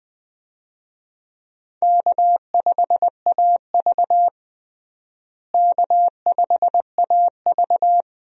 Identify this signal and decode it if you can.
Morse code, heard as a keyed tone, about 21 wpm, 700 Hz. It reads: K5AV K5AV
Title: USA Call Signs - 4 Characters - Repeated 3x - 20wpm